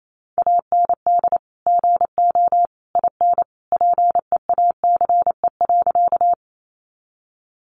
Morse code, heard as a keyed tone, about 28 wpm, 700 Hz. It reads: AND GO IN PEACE.
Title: Sentences from Top 1500 Words 28wpm